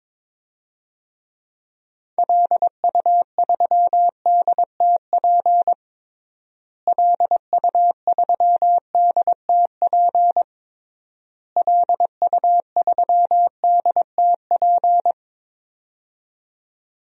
Morse code, heard as a keyed tone, about 22 wpm, 700 Hz. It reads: LU3DTP LU3DTP LU3DTP
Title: DXCC-100 Call Signs - 6 Characters - Repeated 3x - 22wpm